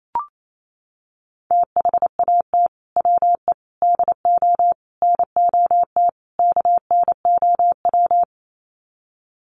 Morse code, heard as a keyed tone, about 28 wpm, 700 Hz. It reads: THAT WE DO NOT KNOW
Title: Sets of 5 Words Encore2 28wpm